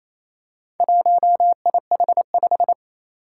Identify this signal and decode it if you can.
Morse code, heard as a keyed tone, about 28 wpm, 700 Hz. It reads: 1IH5
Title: Four Challenging Letter-Number Combinations 28wpm